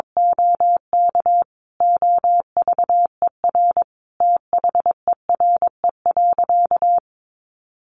Morse code, heard as a keyed tone, about 22 wpm, 700 Hz. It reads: OK OVER THERE.
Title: Sentences from Top 1200 Words 22wpm